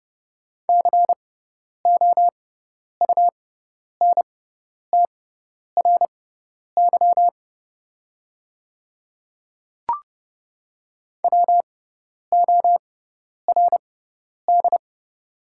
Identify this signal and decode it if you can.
Morse code, heard as a keyed tone, about 30 wpm, spaced as for 10 wpm, 700 Hz. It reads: COUNTRY WORD
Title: 6x Character Spacing - Top 300 Words - 30wpm